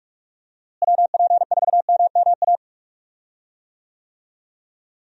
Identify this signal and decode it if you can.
Morse code, heard as a keyed tone, about 41 wpm, 700 Hz. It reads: WP4MMA
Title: USA Call Signs - 6 Characters - Repeated 2x - 45wpm